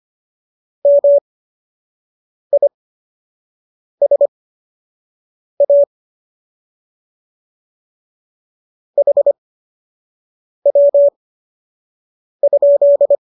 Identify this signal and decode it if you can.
Morse code, heard as a keyed tone, about 25 wpm, spaced as for 6 wpm, 570 Hz. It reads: MISA HW?